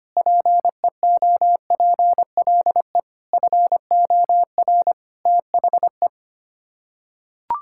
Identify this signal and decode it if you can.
Morse code, heard as a keyed tone, about 25 wpm, 700 Hz. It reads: PEOPLE FOR THE
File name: Sets of 9 Words - 25wpm